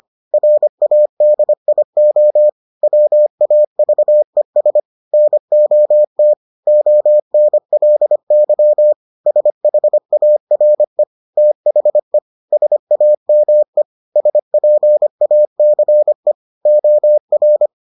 Morse code, heard as a keyed tone, about 25 wpm, 600 Hz. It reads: RADIO WAVES NOT ONLY SHARE THE SAME SPACE OR